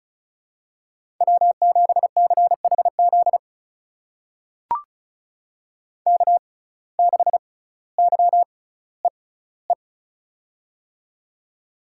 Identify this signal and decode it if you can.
Morse code, heard as a keyed tone, about 36 wpm, spaced as for 18 wpm, 700 Hz. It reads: W7CHZ K6YEE
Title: USA Call Signs - 5 Characters - 6x Character Spacing - 35wpm